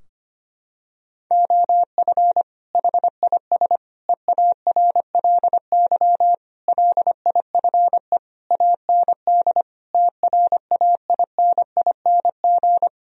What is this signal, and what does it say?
Morse code, heard as a keyed tone, about 25 wpm, 700 Hz. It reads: OF HIS EARLY LIFE AND TRAINING
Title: Sentences from Top 1000 Words - Review Part 4 - 25wpm